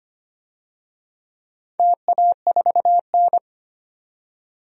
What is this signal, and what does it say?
Morse code, heard as a keyed tone, about 25 wpm, 700 Hz. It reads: TA4N
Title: DXCC-100 Call Signs - 4 Characters - 25wpm